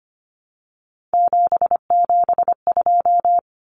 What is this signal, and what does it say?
Morse code, heard as a keyed tone, about 25 wpm, 700 Hz. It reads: 772
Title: Numbers - 3-Digit 25wpm